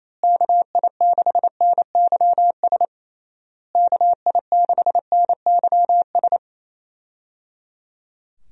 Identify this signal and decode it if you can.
Morse code, heard as a keyed tone, about 28 wpm, 700 Hz. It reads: KI6NYS KI6NYS
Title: USA Call Signs - 6 Characters - Repeated 3x - 28wpm